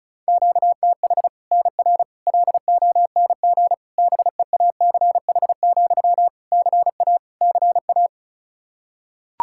Morse code, heard as a keyed tone, about 34 wpm, 700 Hz. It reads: QTH NR LONG BEACH, CA CA
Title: K1USN SST QSO Practice 35wpm